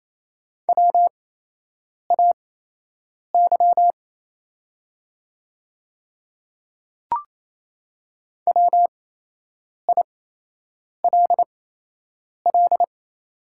Morse code, heard as a keyed tone, about 28 wpm, spaced as for 8 wpm, 700 Hz. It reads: WAY WILL
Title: 8x Character Spacing - Top 100 Words - 28wpm